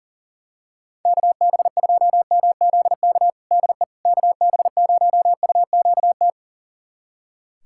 Morse code, heard as a keyed tone, about 40 wpm, 700 Hz. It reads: KB2MZK DE KB0UQT
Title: USA Callsign DE Callsign 40wpm